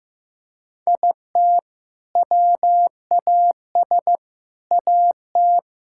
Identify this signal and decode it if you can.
Morse code, heard as a keyed tone, about 15 wpm, 700 Hz. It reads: IT WAS AT
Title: Sentences from Top 1000 Words - Review Part 5 - 15wpm